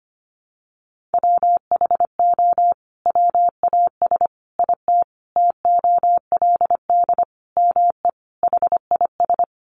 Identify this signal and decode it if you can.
Morse code, heard as a keyed tone, about 25 wpm, 700 Hz. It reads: WHO WAS IT TOLD ME HIS
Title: Sentences from Top 400 Words 25wpm